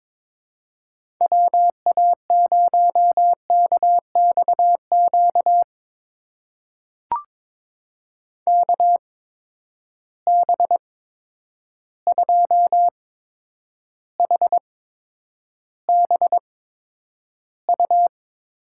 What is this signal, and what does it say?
Morse code, heard as a keyed tone, about 22 wpm, spaced as for 9 wpm, 700 Hz. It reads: WA0KXQ KB2HBU
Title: USA Call Signs - 6 Characters - 8x Character Spacing - 22wpm